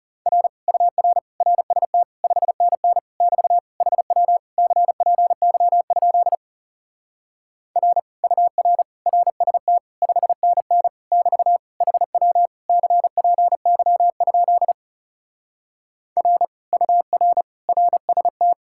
Morse code, heard as a keyed tone, about 37 wpm, 700 Hz. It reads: R UR RST 5NN <BT> HW CPY? R UR RST 5NN <BT> HW CPY? R UR RST
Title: Speed-Racing Common QSO Phrases - 30wpm